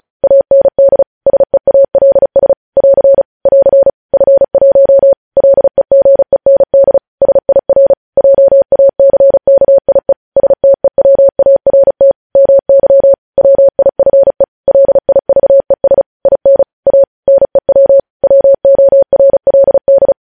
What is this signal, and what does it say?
Morse code, heard as a keyed tone, about 36 wpm, 550 Hz. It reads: AND SEALS <AR> <AR> F1 LEGEND SIR JACKIE STEWART MY WIFE LIVES IN A NEW WORLD
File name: podcast / News headlines-1x-35x35